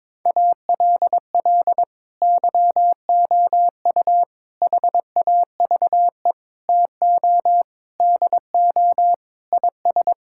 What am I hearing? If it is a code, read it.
Morse code, heard as a keyed tone, about 22 wpm, 700 Hz. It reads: ALL YOU HAVE TO DO IS
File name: Sets of 9 Words - 22wpm